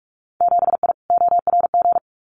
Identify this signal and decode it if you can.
Morse code, heard as a keyed tone, about 44 wpm, 700 Hz. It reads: 7I ORG
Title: Winter Field Day Exchanges Encore 45wpm